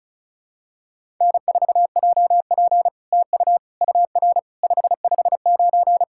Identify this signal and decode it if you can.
Morse code, heard as a keyed tone, about 36 wpm, 700 Hz. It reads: N4JP TU UR 559
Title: SOTA QSO Practice 35wpm